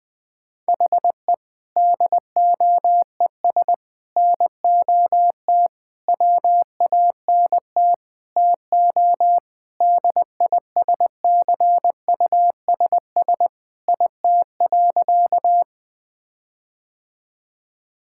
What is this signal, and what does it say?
Morse code, heard as a keyed tone, about 20 wpm, 700 Hz. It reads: HE DOES NOT WANT TO DISCUSS IT.
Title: Sentences with 5-Word Phrases - 20wpm